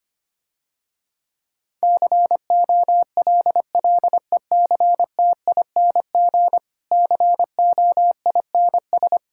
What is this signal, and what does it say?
Morse code, heard as a keyed tone, about 25 wpm, 700 Hz. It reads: COLLECTING COINS
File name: practice / 25wpm / Difficult Sentences with Hints - 25wpm